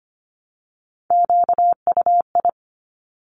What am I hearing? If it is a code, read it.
Morse code, heard as a keyed tone, about 25 wpm, 700 Hz. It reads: QUI
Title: Top Three Letter Combinations in Words 25wpm